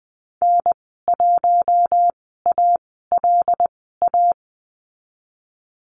Morse code, heard as a keyed tone, about 20 wpm, spaced as for 15 wpm, 700 Hz. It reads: N1ALA
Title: 122 Lesson 20g - Number 3 - Call Signs 20wpm